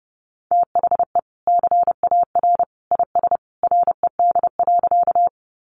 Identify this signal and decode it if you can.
Morse code, heard as a keyed tone, about 30 wpm, 700 Hz. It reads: THE CAR IS RED.
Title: Easy Phrases 30wpm